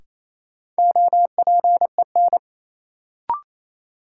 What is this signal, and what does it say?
Morse code, heard as a keyed tone, about 28 wpm, 700 Hz. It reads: OPEN
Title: Sentences from Top 300 Words - New Word Drill - 28wpm